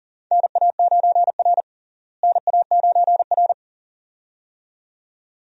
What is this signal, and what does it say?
Morse code, heard as a keyed tone, about 40 wpm, 700 Hz. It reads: NA9R NA9R
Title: USA Call Signs - 4 Characters - Repeated 2x - 40wpm